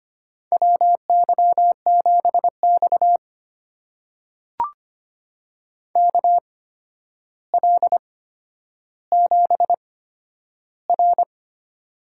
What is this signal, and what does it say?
Morse code, heard as a keyed tone, about 25 wpm, spaced as for 11 wpm, 700 Hz. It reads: WY7X KL7R
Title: USA Call Signs - 4 Characters - 8x Character Spacing - 25wpm